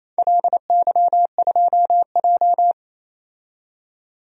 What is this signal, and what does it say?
Morse code, heard as a keyed tone, about 28 wpm, 700 Hz. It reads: LY2J
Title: DXCC-100 Call Signs - 4 Characters - 28wpm